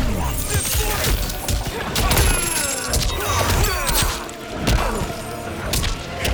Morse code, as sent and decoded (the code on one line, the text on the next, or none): none